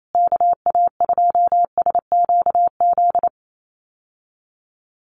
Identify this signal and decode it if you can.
Morse code, heard as a keyed tone, about 28 wpm, 700 Hz. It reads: KA2SQZ
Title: USA Call Signs Repeated 2x 28wpm